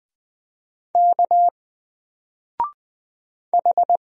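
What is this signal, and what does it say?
Morse code, heard as a keyed tone, about 20 wpm, 700 Hz. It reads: K H
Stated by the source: Single letter and number Morse drill at 20 wpm, from N4LCD.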